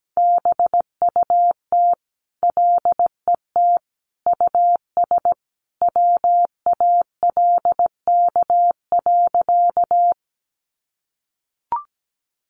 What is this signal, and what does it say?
Morse code, heard as a keyed tone, about 17 wpm, 700 Hz. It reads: BUT LET US WALK.
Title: Sentences from Top 300 Words 17wpm